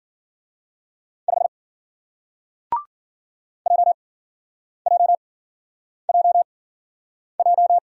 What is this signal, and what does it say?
Morse code, heard as a keyed tone, about 53 wpm, spaced as for 9 wpm, 700 Hz. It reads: 5 JJJJ